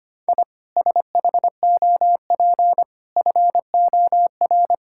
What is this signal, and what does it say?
Morse code, heard as a keyed tone, about 25 wpm, 690 Hz. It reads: I SHOP FOR
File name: Easy Sentences with Hints - 25wpm